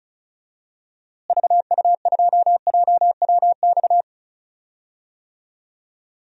Morse code, heard as a keyed tone, about 36 wpm, 700 Hz. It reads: VU2JWX
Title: DXCC-100 Call Signs - 6 Characters - Repeated 2x - 35wpm